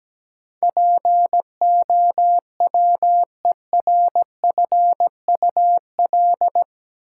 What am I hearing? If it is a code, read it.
Morse code, heard as a keyed tone, about 17 wpm, 700 Hz. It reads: POWERFUL
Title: Sentences from Top 1300 Words - New Word Drill Alt - 17wpm